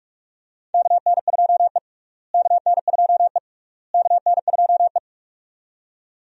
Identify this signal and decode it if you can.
Morse code, heard as a keyed tone, about 42 wpm, 700 Hz. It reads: KN2E KN2E KN2E
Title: USA Call Signs - 4 Characters - Repeated 3x - 45wpm